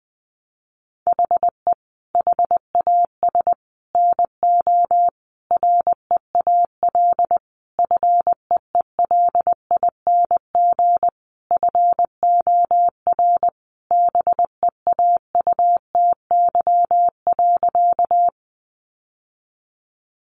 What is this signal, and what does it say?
Morse code, heard as a keyed tone, about 20 wpm, 700 Hz. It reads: HE HAS NO REAL FEELING FOR BEAUTY.
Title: Sentences from Top 1200 Words 20wpm